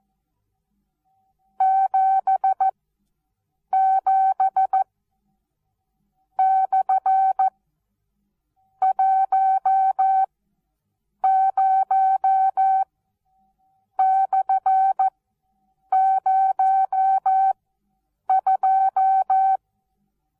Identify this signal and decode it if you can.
Morse code, heard as a keyed tone, about 14 wpm, spaced as for 6 wpm, 770 Hz. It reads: 77/10/02
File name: Onlinelehrgang / morsen / DJ4UF Morsen19